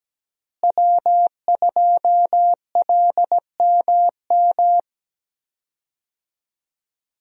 Morse code, heard as a keyed tone, about 17 wpm, 700 Hz. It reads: W2LMM
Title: USA Call Signs - 5 Characters - Repeated 2x - 17wpm